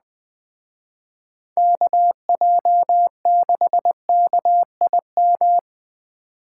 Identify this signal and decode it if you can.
Morse code, heard as a keyed tone, about 20 wpm, 700 Hz. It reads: KJ6KIM